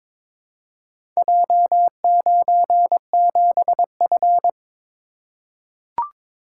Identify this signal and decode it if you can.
Morse code, heard as a keyed tone, about 22 wpm, 700 Hz. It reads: J97F
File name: practice / 22wpm / Four Letter-Number Combinations - 22wpm